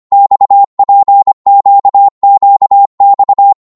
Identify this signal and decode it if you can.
Morse code, heard as a keyed tone, about 25 wpm, 800 Hz. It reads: XPQQX
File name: Morse / ltr-25wpm